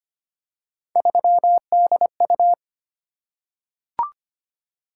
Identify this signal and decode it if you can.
Morse code, heard as a keyed tone, about 25 wpm, 700 Hz. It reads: 3DU
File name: Three Letter-Number Combinations - 25wpm